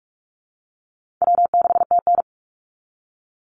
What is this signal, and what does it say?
Morse code, heard as a keyed tone, about 44 wpm, 700 Hz. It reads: W6TN
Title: Easy Call Signs Repeated 3x - 45wpm